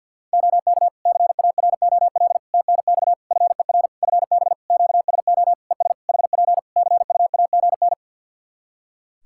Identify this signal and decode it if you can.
Morse code, heard as a keyed tone, about 50 wpm, 700 Hz. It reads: OK CAROL TNX FER FB QSO ES HP CUAGN
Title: Practice QSO - Exchange 4 - Ending - 50wpm